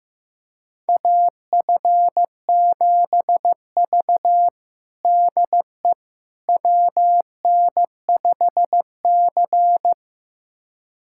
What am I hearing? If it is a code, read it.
Morse code, heard as a keyed tone, about 15 wpm, 700 Hz. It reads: AF7V DE WN5C
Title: USA Callsign DE Callsign - 4 Characters - 15wpm